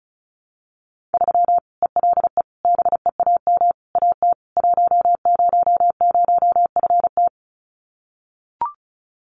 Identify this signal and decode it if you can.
Morse code, heard as a keyed tone, about 34 wpm, 700 Hz. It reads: 3 ELE BEAM AT 100FT
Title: Common QSO Phrases 35wpm